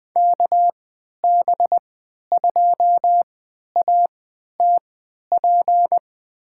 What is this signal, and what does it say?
Morse code, heard as a keyed tone, about 20 wpm, spaced as for 11 wpm, 700 Hz. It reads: KB2ATP
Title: USA Call Signs - 6 Characters - 3x Character Spacing - 20wpm